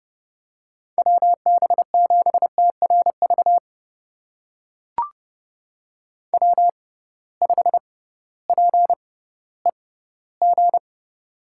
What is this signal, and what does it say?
Morse code, heard as a keyed tone, about 32 wpm, spaced as for 17 wpm, 700 Hz. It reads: WB7TRV W5PEG
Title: USA Call Signs - 6x Character Spacing - 30wpm